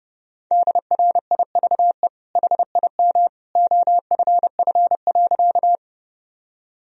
Morse code, heard as a keyed tone, about 32 wpm, 700 Hz. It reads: DRIVE HIM OFF.
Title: Sentences from Top 1000 Words - Review Part 2 - 30wpm